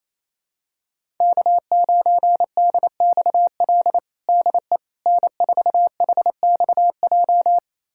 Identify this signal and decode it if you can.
Morse code, heard as a keyed tone, about 28 wpm, 700 Hz. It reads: K9DXL DE N4HXJ
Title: USA Callsign DE Callsign - 5 Characters - 28wpm